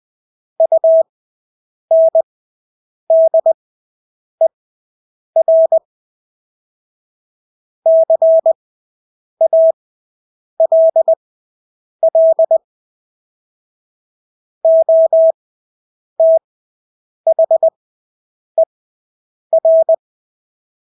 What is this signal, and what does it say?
Morse code, heard as a keyed tone, about 20 wpm, spaced as for 8 wpm, 650 Hz. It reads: UNDER CALL OTHER
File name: Engelske ord 20/8 wpm